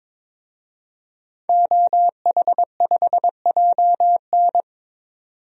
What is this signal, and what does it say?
Morse code, heard as a keyed tone, about 21 wpm, 700 Hz. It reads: OH5JN
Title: DXCC-100 Call Signs - 5 Characters - 22wpm